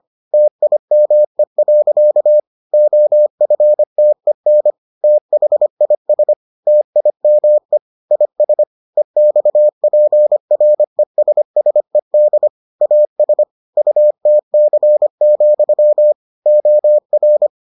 Morse code, heard as a keyed tone, about 25 wpm, 600 Hz. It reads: TIME. OFTEN THIS TIME IS EXPRESSED AS UTC, OR